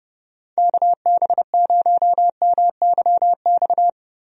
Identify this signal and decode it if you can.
Morse code, heard as a keyed tone, about 30 wpm, 700 Hz. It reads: KB0MYX
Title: USA Call Signs - 6 Characters - Repeated 3x - 30wpm